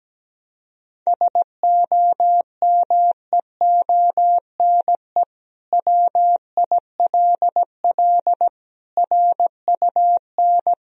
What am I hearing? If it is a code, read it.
Morse code, heard as a keyed tone, about 17 wpm, 700 Hz. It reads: SOMEONE WILL RUN
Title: Sentences from Top 1000 Words - Review Part 1 - 17wpm